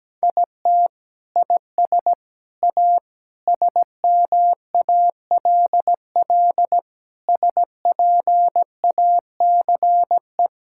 Morse code, heard as a keyed tone, about 17 wpm, 700 Hz. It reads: IT IS A SMALL SPACE
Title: Sentences from Top 1000 Words - Review Part 5 - 17wpm